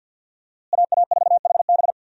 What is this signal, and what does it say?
Morse code, heard as a keyed tone, about 50 wpm, 700 Hz. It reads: AA4HB
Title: USA Call Signs - 5 Characters - Repeated 2x - 50wpm